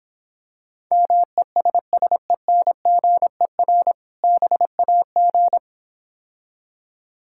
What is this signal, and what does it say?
Morse code, heard as a keyed tone, about 26 wpm, 700 Hz. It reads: MESSENGER BAG